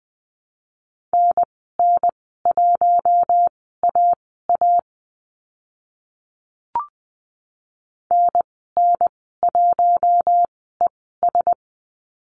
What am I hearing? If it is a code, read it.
Morse code, heard as a keyed tone, about 20 wpm, spaced as for 14 wpm, 700 Hz. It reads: NN1AA NN1ES
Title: 042 Lesson 8g - Number 1 - Call Signs 20wpm